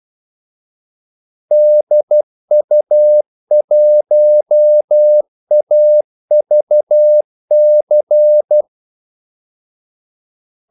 Morse code, heard as a keyed tone, about 12 wpm, 600 Hz. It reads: DU1AVC
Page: 2 ekstra ordmellemrum mellem kaldesignalerne